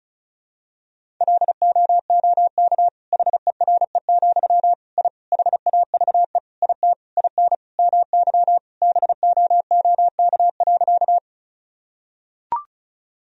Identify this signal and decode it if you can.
Morse code, heard as a keyed tone, about 36 wpm, 700 Hz. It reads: LOOK HERE, I HAVE IT IN MY BOOK.
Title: Sentences from Top 1000 Words - Review Part 1 - 35wpm